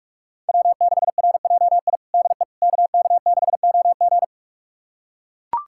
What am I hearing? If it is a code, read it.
Morse code, heard as a keyed tone, about 42 wpm, 700 Hz. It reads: W6RJI DE KK6OG
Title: USA Callsign DE Callsign - 5 Characters - 45wpm